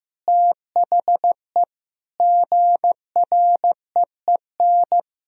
Morse code, heard as a keyed tone, about 16 wpm, 700 Hz. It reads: THE GREEN
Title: Sentences from Top 1000 Words - Review Part 3 - 15wpm